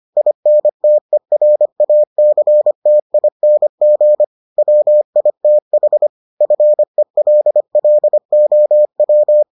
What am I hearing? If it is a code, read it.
Morse code, heard as a keyed tone, about 25 wpm, 600 Hz. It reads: INTERACTING WITH FELLOW